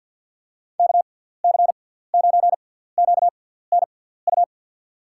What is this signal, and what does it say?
Morse code, heard as a keyed tone, about 49 wpm, spaced as for 17 wpm, 700 Hz. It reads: KC9QNU
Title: USA Call Signs - 6 Characters - 6x Character Spacing - 50wpm